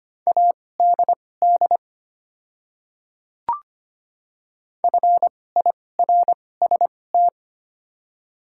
Morse code, heard as a keyed tone, about 25 wpm, spaced as for 18 wpm, 700 Hz. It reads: ADD FIRST
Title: 2x Character Spacing - Top 200 Words - 25wpm